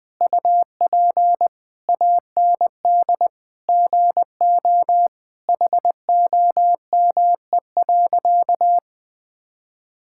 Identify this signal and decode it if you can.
Morse code, heard as a keyed tone, about 19 wpm, 700 Hz. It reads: UP AND GO HOME.
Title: Sentences from Top 1000 Words - Review Part 6 - 20wpm